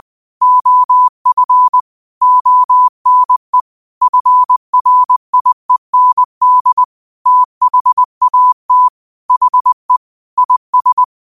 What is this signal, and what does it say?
Morse code, heard as a keyed tone, about 20 wpm, 1000 Hz. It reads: OF ONE FRIEND THAT HE IS
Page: New quotes every day in morse code at 20 Words per minute.